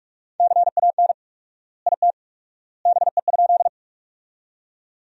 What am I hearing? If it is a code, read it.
Morse code, heard as a keyed tone, about 46 wpm, 700 Hz. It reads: CAN IT BE?
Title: 4x Word-Spacing - Sentences from Top 100 Words - 45wpm